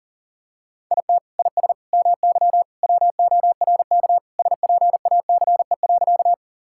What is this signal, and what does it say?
Morse code, heard as a keyed tone, about 37 wpm, 700 Hz. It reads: IT IS MY WORK SPACE.
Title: Sentences from Top 1000 Words - Review Part 5 - 40wpm